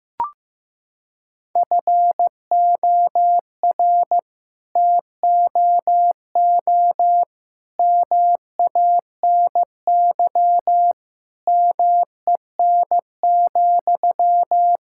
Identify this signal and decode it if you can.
Morse code, heard as a keyed tone, about 16 wpm, 700 Hz. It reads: FOR TOO MANY MEN,
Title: Sentences from Top 1000 Words - Review Part 6 - 15wpm